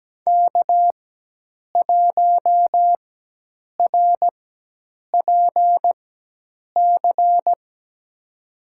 Morse code, heard as a keyed tone, about 17 wpm, spaced as for 8 wpm, 700 Hz. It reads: K1RPC
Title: USA Call Signs - 4x Character Spacing - 17wpm